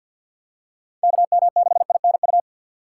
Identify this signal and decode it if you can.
Morse code, heard as a keyed tone, about 50 wpm, 700 Hz. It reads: KM6INU
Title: USA Call Signs - 3x Character Spacing - 50wpm